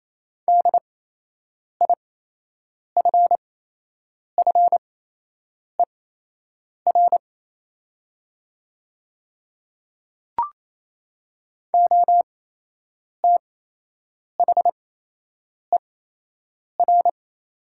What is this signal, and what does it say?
Morse code, heard as a keyed tone, about 28 wpm, spaced as for 8 wpm, 700 Hz. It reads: DIFFER OTHER
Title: 8x Character Spacing - Top 200 Words - 28wpm